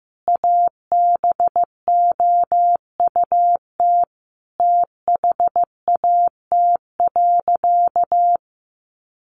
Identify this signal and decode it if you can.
Morse code, heard as a keyed tone, about 15 wpm, 700 Hz. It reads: ABOUT THAT.
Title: Sentences from Top 200 Words 15wpm